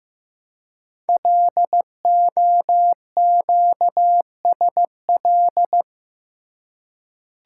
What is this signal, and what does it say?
Morse code, heard as a keyed tone, about 15 wpm, 700 Hz. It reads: LOQSL
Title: Code Groups - 5 Letters - 15wpm